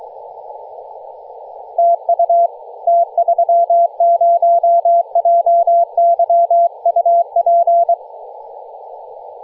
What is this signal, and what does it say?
Morse code, heard as a keyed tone, about 23 wpm, 680 Hz. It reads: TU T30JY UP
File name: posts / 2016-nov-13-0348z-t30jy-28-034mhz-cw